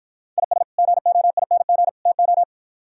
Morse code, heard as a keyed tone, about 53 wpm, 700 Hz. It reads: IS GOING TO